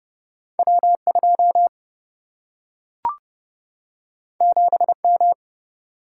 Morse code, heard as a keyed tone, about 30 wpm, 700 Hz. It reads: W2 7M
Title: Single-Letter Single-Number 30wpm